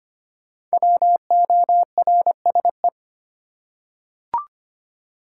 Morse code, heard as a keyed tone, about 25 wpm, 700 Hz. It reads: WORSE
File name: 5 Letter Words - First 3 Letters Given - 25wpm